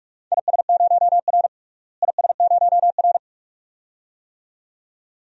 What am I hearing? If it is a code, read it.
Morse code, heard as a keyed tone, about 46 wpm, 700 Hz. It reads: IS0R IS0R
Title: DXCC-100 Call Signs - 4 Characters - Repeated 2x - 45wpm